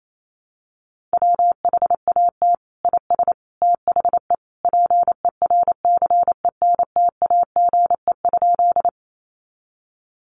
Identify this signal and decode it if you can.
Morse code, heard as a keyed tone, about 28 wpm, 700 Hz. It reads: WHAT IS THE PERCENTAGE?
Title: Sentences from Top 1300 Words 28wpm